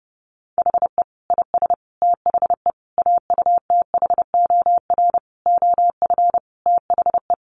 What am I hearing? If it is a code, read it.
Morse code, heard as a keyed tone, about 30 wpm, 700 Hz. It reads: HE IS THE AUTHOR OF THE
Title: Sentences from Top 1200 Words 30wpm